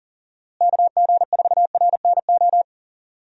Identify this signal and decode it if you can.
Morse code, heard as a keyed tone, about 37 wpm, 700 Hz. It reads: KG4RNO
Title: USA Call Signs - 6 Characters - Repeated 3x - 40wpm